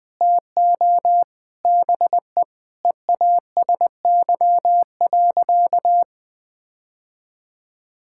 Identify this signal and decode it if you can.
Morse code, heard as a keyed tone, about 20 wpm, 700 Hz. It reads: TO BE EASY.
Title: Sentences with 5-Word Phrases - 20wpm